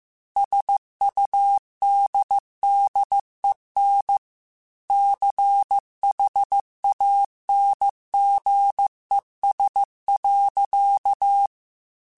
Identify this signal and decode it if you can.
Morse code, heard as a keyed tone, about 15 wpm, 800 Hz. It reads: SUDDEN CHANGES.
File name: Technical Subjects / Technical Subjects-mp3s / Reactance @13